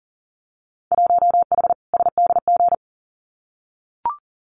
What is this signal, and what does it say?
Morse code, heard as a keyed tone, about 40 wpm, 700 Hz. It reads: 1H SDG
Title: Winter Field Day Exchanges 40wpm